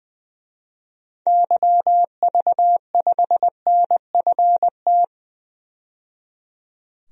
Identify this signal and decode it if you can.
Morse code, heard as a keyed tone, about 21 wpm, 700 Hz. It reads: YV5NFT
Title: DXCC-100 Call Signs - 6 Characters - 20wpm